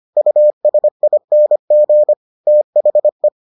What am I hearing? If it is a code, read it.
Morse code, heard as a keyed tone, about 25 wpm, 600 Hz. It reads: USING THE